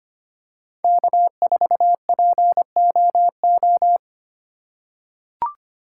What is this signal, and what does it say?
Morse code, heard as a keyed tone, about 25 wpm, 700 Hz. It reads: K4POO
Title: USA Call Signs - 5 Characters - 4x Character Spacing - 25wpm